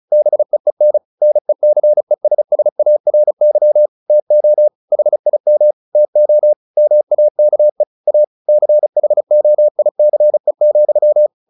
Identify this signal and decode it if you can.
Morse code, heard as a keyed tone, about 35 wpm, 600 Hz. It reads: BEEN NECESSARY TO HIM TO MAKE A CHOICE,
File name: the old man and the sea / the old man and the sea 1 35wpm